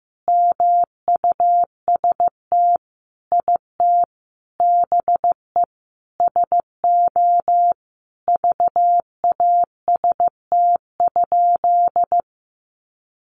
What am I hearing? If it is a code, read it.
Morse code, heard as a keyed tone, about 15 wpm, 700 Hz. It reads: MUST IT BE SO VAST?
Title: Sentences from Top 1300 Words 15wpm